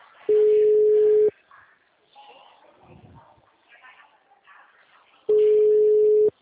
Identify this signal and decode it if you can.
Ringback tone (Europe, 425 Hz).